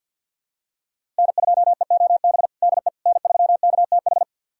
Morse code, heard as a keyed tone, about 49 wpm, 700 Hz. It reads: N2EOB DE N3KTH
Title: USA Callsign DE Callsign - 5 Characters - 50wpm